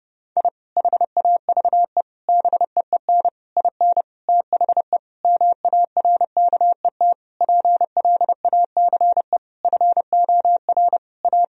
Morse code, heard as a keyed tone, about 31 wpm, 700 Hz. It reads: I HAVE BEEN IN THE MARKET PLACE FOR A
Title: Sentences from Top 1000 Words - Review Part 3 - 30wpm